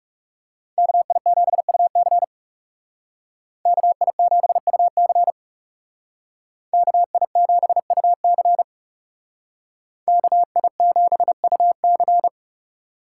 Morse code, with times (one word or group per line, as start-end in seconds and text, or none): 0.8-2.2 KI7UC
3.7-5.3 KI7UC
6.7-8.6 KI7UC
10.1-12.3 KI7UC